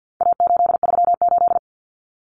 Morse code, heard as a keyed tone, about 50 wpm, 700 Hz. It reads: A838
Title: Common Cut Numbers - 4-Digit 50wpm